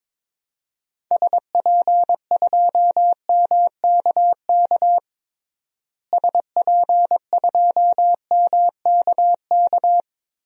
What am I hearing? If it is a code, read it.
Morse code, heard as a keyed tone, about 22 wpm, 700 Hz. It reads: SP2MKK SP2MKK
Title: DXCC-100 Call Signs - 6 Characters - Repeated 3x - 22wpm